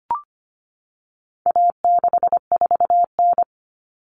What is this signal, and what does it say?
Morse code, heard as a keyed tone, about 25 wpm, 700 Hz. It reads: A64N
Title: Common Cut Numbers - 4-Digit 25wpm